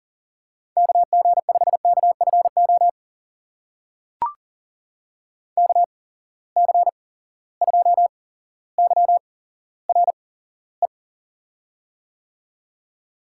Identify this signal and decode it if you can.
Morse code, heard as a keyed tone, about 38 wpm, spaced as for 17 wpm, 700 Hz. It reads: KG5KFO KC2YRE
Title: USA Call Signs - 6 Characters - 8x Character Spacing - 40wpm